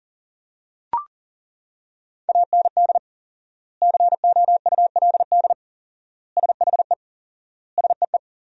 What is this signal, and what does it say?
Morse code, heard as a keyed tone, about 40 wpm, 700 Hz. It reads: AND COULD SHE SEE
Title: 4x Word-Spacing - Sentences from Top 100 Words - 40wpm